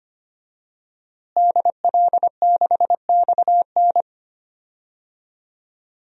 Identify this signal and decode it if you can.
Morse code, heard as a keyed tone, about 25 wpm, 700 Hz. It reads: DL6XN
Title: DXCC-100 Call Signs - 5 Characters - Repeated 3x - 25wpm